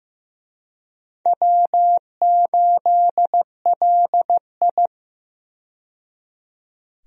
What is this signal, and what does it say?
Morse code, heard as a keyed tone, about 15 wpm, 700 Hz. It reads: W8LI
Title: USA Call Signs - 4 Characters - 15wpm